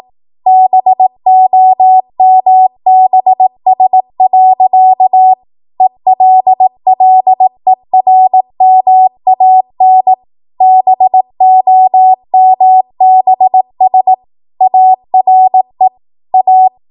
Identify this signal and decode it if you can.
Morse code, heard as a keyed tone, about 18 wpm, 750 Hz. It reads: BOMBS. ELLERMAN BOMBS ARE A
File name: Morse / 18WPM